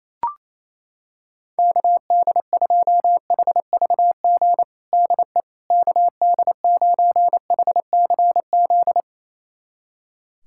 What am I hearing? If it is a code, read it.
Morse code, heard as a keyed tone, about 28 wpm, 700 Hz. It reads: KD2HVG DE KD9HCZ
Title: USA Callsign DE Callsign - 6 Characters - 28wpm